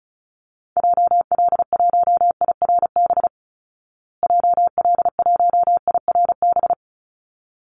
Morse code, heard as a keyed tone, about 34 wpm, 700 Hz. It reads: JL1IRB JL1IRB
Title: Call Signs Repeated 3x - 35wpm